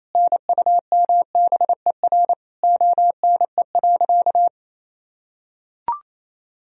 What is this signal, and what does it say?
Morse code, heard as a keyed tone, about 28 wpm, 700 Hz. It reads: NUMBER ONE.
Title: Sentences from Top 1000 Words - Review Part 1 - 28wpm